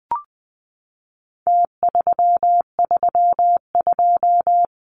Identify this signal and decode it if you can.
Morse code, heard as a keyed tone, about 20 wpm, 700 Hz. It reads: T332
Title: Numbers - Contest Style - 4-Digit 20wpm